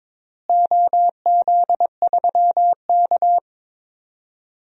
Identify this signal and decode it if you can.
Morse code, heard as a keyed tone, about 22 wpm, 700 Hz. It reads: OZ3K
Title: DXCC-100 Call Signs - 4 Characters - Repeated 2x - 22wpm